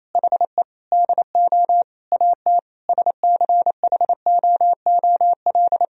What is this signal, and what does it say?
Morse code, heard as a keyed tone, about 28 wpm, 700 Hz. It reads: HE DO AT SCHOOL
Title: Sentences from Top 1000 Words - Review Part 1 - 28wpm